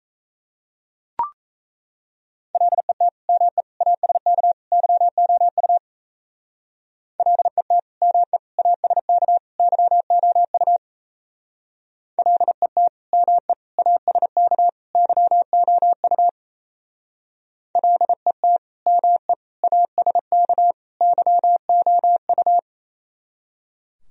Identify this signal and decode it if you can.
Morse code, heard as a keyed tone, about 34 wpm, 700 Hz. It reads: LET ME ASK YOU LET ME ASK YOU LET ME ASK YOU LET ME ASK YOU
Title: Speed-Racing - Sets of 4 Words 28wpm